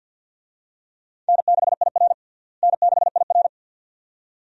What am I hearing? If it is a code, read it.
Morse code, heard as a keyed tone, about 47 wpm, 700 Hz. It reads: N6IR N6IR
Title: USA Call Signs - 4 Characters - Repeated 2x - 50wpm